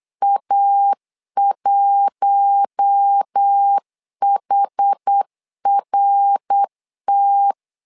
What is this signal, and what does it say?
Morse code, heard as a keyed tone, about 8 wpm, 800 Hz. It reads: A1HRT
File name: LEZIONI / 4 Q1 8wpm 3035